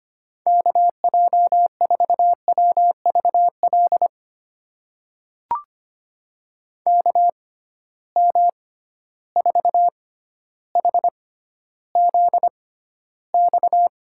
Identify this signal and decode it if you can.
Morse code, heard as a keyed tone, about 25 wpm, spaced as for 13 wpm, 700 Hz. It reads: KJ4WVL KM4HZX
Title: USA Call Signs - 6 Characters - 6x Character Spacing - 25wpm